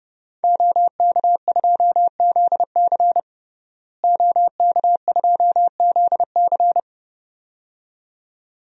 Morse code, heard as a keyed tone, about 30 wpm, 700 Hz. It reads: OK2ZC OK2ZC
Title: DXCC-100 Call Signs - 5 Characters - Repeated 2x - 30wpm